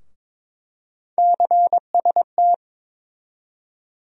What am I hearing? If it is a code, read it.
Morse code, heard as a keyed tone, about 21 wpm, 700 Hz. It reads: CST
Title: USA Call Sign Postfixes - 3 Letters 22wpm